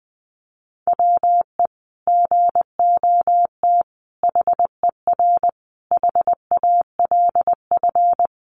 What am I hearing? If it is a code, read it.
Morse code, heard as a keyed tone, about 20 wpm, 700 Hz. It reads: WE GOT HER HALF
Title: Sentences from Top 400 Words 20wpm